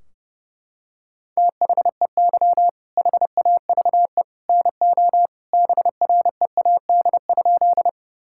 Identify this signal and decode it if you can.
Morse code, heard as a keyed tone, about 30 wpm, 700 Hz. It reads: THEY HAVE NO BREAD?
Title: Sentences from Top 1600 Words 30wpm